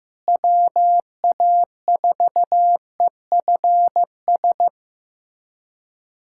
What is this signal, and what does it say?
Morse code, heard as a keyed tone, about 15 wpm, 700 Hz. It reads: WA4EFS
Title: USA Callsign DE Callsign - 6 Characters - 15wpm